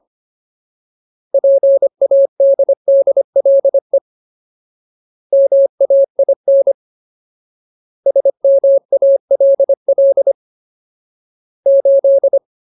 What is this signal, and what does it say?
Morse code, heard as a keyed tone, about 25 wpm, 550 Hz. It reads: PADDLE MAIN SMALL 8
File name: Ultimate Word List / 25WPM Ultimate